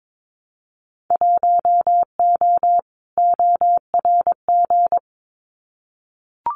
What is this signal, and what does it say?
Morse code, heard as a keyed tone, about 22 wpm, 700 Hz. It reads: 1O ORG
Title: Winter Field Day Exchanges 22wpm